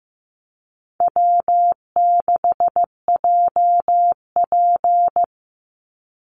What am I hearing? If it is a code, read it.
Morse code, heard as a keyed tone, about 15 wpm, 700 Hz. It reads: W6JP
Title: NA SOTA Chaser Callsigns 15wpm